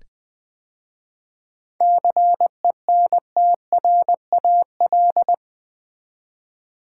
Morse code, heard as a keyed tone, about 21 wpm, 700 Hz. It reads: CENTRAL
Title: 7 Letter Words - First 3 Letters Given - 20wpm